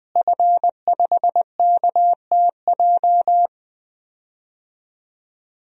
Morse code, heard as a keyed tone, about 20 wpm, 700 Hz. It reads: F5KTJ
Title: DXCC-100 Call Signs - 5 Characters - Repeated 3x - 20wpm